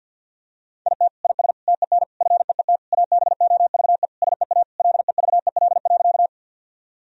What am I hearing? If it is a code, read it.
Morse code, heard as a keyed tone, about 50 wpm, 700 Hz. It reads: IT IS TEN FEET ABOVE SEA LEVEL.
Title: Sentences from Top 1000 Words - Review Part 4 - 50wpm